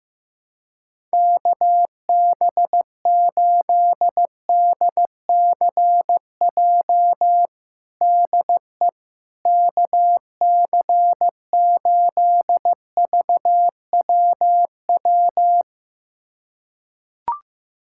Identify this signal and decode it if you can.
Morse code, heard as a keyed tone, about 15 wpm, 700 Hz. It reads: KB8DCJ DE KC8VWW
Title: USA Callsign DE Callsign - 6 Characters - 15wpm